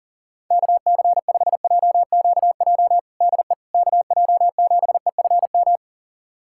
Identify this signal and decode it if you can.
Morse code, heard as a keyed tone, about 40 wpm, 700 Hz. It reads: KC5JQJ DE KJ7EFM
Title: USA Callsign DE Callsign - 6 Characters - 40wpm